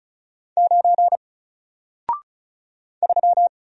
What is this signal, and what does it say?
Morse code, heard as a keyed tone, about 34 wpm, 700 Hz. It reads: 9 3
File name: Single-Letter-Number-35wpm